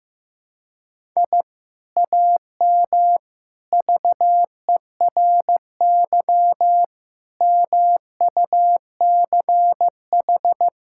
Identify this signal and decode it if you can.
Morse code, heard as a keyed tone, about 15 wpm, 700 Hz. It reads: I AM VERY MUCH
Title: Sentences from Top 1000 Words - Review Part 6 - 15wpm